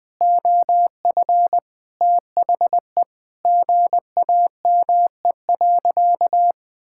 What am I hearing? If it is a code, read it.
Morse code, heard as a keyed tone, about 21 wpm, 700 Hz. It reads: OF THE GAME.
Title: Sentences with 5-Word Phrases - 20wpm